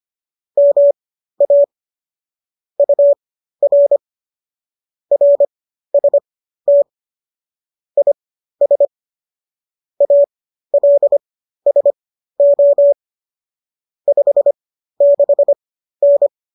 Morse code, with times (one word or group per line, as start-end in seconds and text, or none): 0.6-1.6 MA
2.8-4.0 UR
5.1-6.8 RST
8.0-8.9 IS
10.0-12.9 ALSO
14.1-16.3 56N